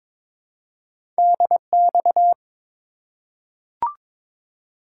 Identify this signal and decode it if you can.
Morse code, heard as a keyed tone, about 22 wpm, 700 Hz. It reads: DX
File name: practice / 22wpm / ARRL Sections v2 - 22wpm